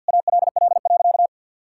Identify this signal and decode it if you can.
Morse code, heard as a keyed tone, about 50 wpm, 700 Hz. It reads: ALL.
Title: Sentences from Top 1600 Words 50wpm